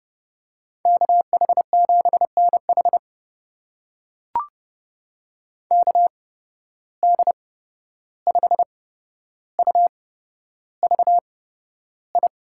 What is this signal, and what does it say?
Morse code, heard as a keyed tone, about 30 wpm, spaced as for 12 wpm, 700 Hz. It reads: KH7NH KD5UVI
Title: USA Call Signs - 8x Character Spacing - 30wpm